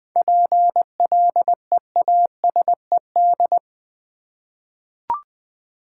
Morse code, heard as a keyed tone, about 20 wpm, 700 Hz. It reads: PLEASED
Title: 7 Letter Words - First 3 Letters Given - 20wpm